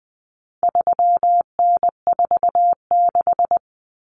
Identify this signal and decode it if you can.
Morse code, heard as a keyed tone, about 20 wpm, 700 Hz. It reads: 3N46
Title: Common Cut Numbers - 4-Digit 20wpm